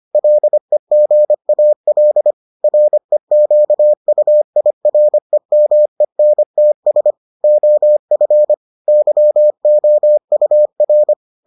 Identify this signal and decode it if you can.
Morse code, heard as a keyed tone, about 25 wpm, 600 Hz. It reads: LEGAL REQUIREMENTS OF YOUR